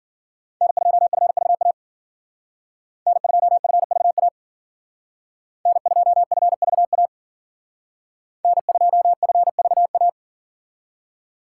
Morse code, heard as a keyed tone, about 52 wpm, 700 Hz. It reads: N2FVA N2FVA N2FVA N2FVA